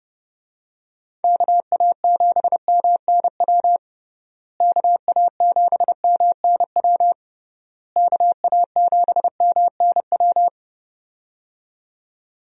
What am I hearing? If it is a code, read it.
Morse code, heard as a keyed tone, about 32 wpm, 700 Hz. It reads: KA7MNW KA7MNW KA7MNW
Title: USA Call Signs - 6 Characters - Repeated 3x - 30wpm